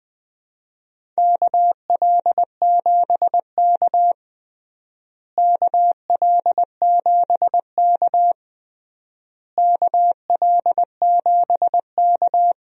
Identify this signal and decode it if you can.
Morse code, heard as a keyed tone, about 20 wpm, 700 Hz. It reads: KL7K KL7K KL7K
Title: DXCC-100 Call Signs - 4 Characters - Repeated 3x - 20wpm